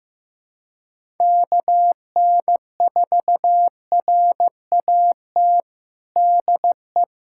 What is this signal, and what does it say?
Morse code, heard as a keyed tone, about 15 wpm, 700 Hz. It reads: KN4RAT DE
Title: USA Callsign DE Callsign - 6 Characters - 15wpm